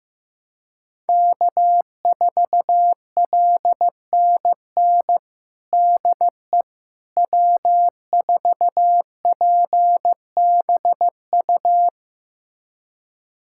Morse code, heard as a keyed tone, about 15 wpm, 700 Hz. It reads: K4LNN DE W4PBU
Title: USA Callsign DE Callsign - 5 Characters - 15wpm